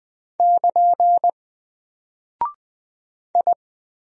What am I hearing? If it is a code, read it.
Morse code, heard as a keyed tone, about 20 wpm, 700 Hz. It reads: <KN> I
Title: Single Character (Letter:Number:Punctuation:Prosign) Alt 20wpm